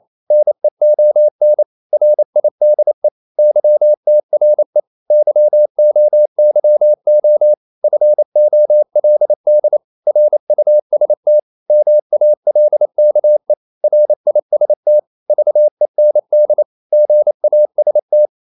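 Morse code, heard as a keyed tone, about 28 wpm, 600 Hz. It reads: NEON RIDE YTRE YOYO FOLD RUST MALKE RIST VEND GAST
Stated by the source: Danish short words sent in Morse at 28 wpm, from CW med Gnister.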